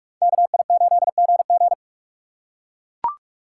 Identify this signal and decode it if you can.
Morse code, heard as a keyed tone, about 43 wpm, 700 Hz. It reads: KI8GG